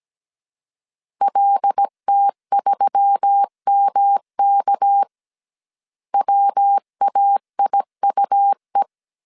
Morse code, heard as a keyed tone, about 17 wpm, 800 Hz. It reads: LT3MK WAIUE
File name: CW / DET / 1 6 1500C S C 16wpm 2733